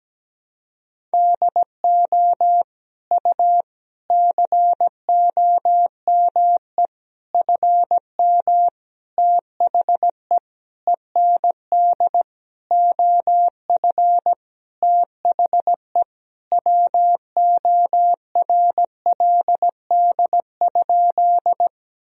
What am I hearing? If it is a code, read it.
Morse code, heard as a keyed tone, about 17 wpm, 700 Hz. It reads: DO U COME FM THE END OF THE WORLD?
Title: Sentences from Top 200 Words and Abbreviations 17wpm